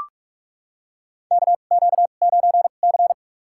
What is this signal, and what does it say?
Morse code, heard as a keyed tone, about 44 wpm, spaced as for 32 wpm, 690 Hz. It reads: KQ9C
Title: USA Call Signs - 4 Characters - 2x Character Spacing - 45wpm